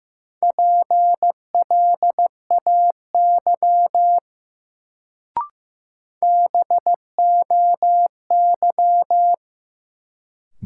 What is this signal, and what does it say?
Morse code, heard as a keyed tone, about 15 wpm, 700 Hz. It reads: PLAY BOY
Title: Sentences from Top 200 Words - New Word Drill - 15wpm